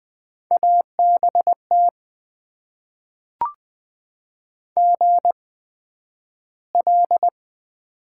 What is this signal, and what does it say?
Morse code, heard as a keyed tone, about 20 wpm, spaced as for 11 wpm, 700 Hz. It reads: ABT GL
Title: Top 90 QSO Elements - 8x Character Spacing - 20wpm